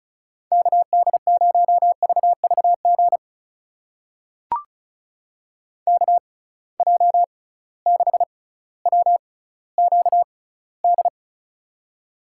Morse code, heard as a keyed tone, about 36 wpm, spaced as for 18 wpm, 700 Hz. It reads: KD0VVG KJ6WQD
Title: USA Call Signs - 6 Characters - 6x Character Spacing - 35wpm